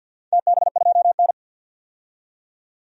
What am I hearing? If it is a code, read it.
Morse code, heard as a keyed tone, about 53 wpm, 700 Hz. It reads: TB2N
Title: DXCC-100 Call Signs - 4 Characters - 50wpm